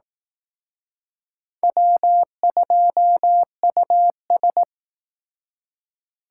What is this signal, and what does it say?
Morse code, heard as a keyed tone, about 19 wpm, 700 Hz. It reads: W2US
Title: Speed Racing - USA 1x2 Call Signs 15wpm